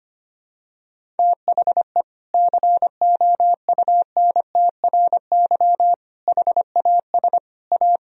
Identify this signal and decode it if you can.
Morse code, heard as a keyed tone, about 25 wpm, 700 Hz. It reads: THE COUNTRY HAS A
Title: Sentences from Top 1000 Words - Review Part 1 - 25wpm